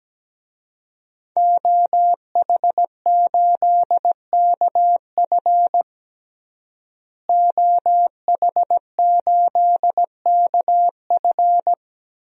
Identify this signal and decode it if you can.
Morse code, heard as a keyed tone, about 17 wpm, 700 Hz. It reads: OH8KF OH8KF
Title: DXCC-100 Call Signs - 5 Characters - Repeated 3x - 17wpm